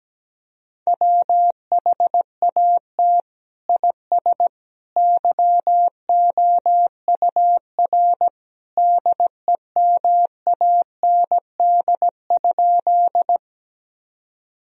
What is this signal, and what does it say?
Morse code, heard as a keyed tone, about 17 wpm, 700 Hz. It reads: WHAT IS YOUR DEMAND?
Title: Sentences from Top 1000 Words - Review Part 4 - 17wpm